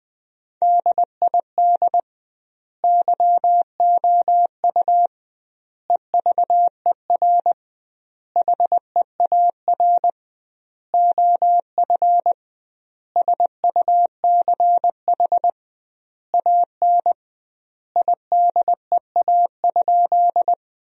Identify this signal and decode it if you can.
Morse code, heard as a keyed tone, about 21 wpm, 700 Hz. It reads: DID YOU EVER HEAR OF SUCH AN IDEA?
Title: 2x Word-Spacing - Sentences from Top 400 Words - 20wpm